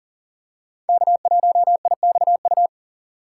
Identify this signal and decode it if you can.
Morse code, heard as a keyed tone, about 40 wpm, 700 Hz. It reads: K1IXU
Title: USA Call Signs - 5 Characters - 4x Character Spacing - 40wpm